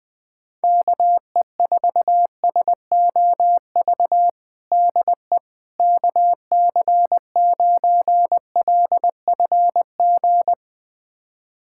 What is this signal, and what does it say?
Morse code, heard as a keyed tone, about 20 wpm, 700 Hz. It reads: KE4SOV DE KC9LFG
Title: USA Callsign DE Callsign - 6 Characters - 20wpm